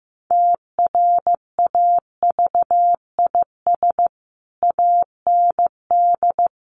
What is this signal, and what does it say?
Morse code, heard as a keyed tone, about 15 wpm, 700 Hz. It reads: TRAVIS AND
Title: A Sound Of Thunder 15wpm